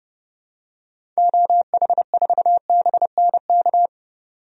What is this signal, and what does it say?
Morse code, heard as a keyed tone, about 30 wpm, 700 Hz. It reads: OH4BNK
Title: DXCC-100 Call Signs - 6 Characters - 30wpm